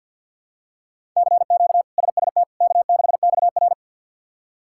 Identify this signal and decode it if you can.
Morse code, heard as a keyed tone, about 52 wpm, 700 Hz. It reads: CQ SST K6XR
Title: K1USN SST QSO Practice - 50wpm